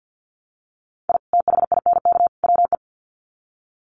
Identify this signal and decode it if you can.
Morse code, heard as a keyed tone, about 50 wpm, 700 Hz. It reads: I THINK WE
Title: Sets of 3 Words 50wpm